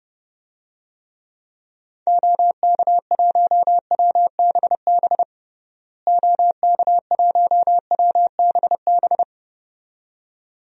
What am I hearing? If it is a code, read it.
Morse code, heard as a keyed tone, about 31 wpm, 700 Hz. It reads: OK1WBB OK1WBB
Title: DXCC-100 Call Signs - 6 Characters - Repeated 2x - 30wpm